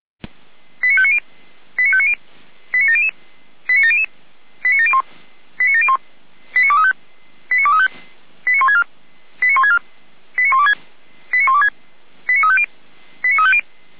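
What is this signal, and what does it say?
Five-tone selective call (ZVEI), heard as quick runs of five tones, tones of about 80 ms. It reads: 89590 89590 89700 89700 89701 89701 89235 89235 89164 89164 89126 89126 89350 89350
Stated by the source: ZVEI five-tone call received by SDR.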